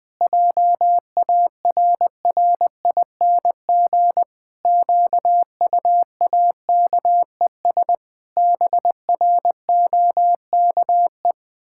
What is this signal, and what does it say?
Morse code, heard as a keyed tone, about 20 wpm, 700 Hz. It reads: JARRING QUAKES BROKE
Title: Pangram Challenge v2 - 20wpm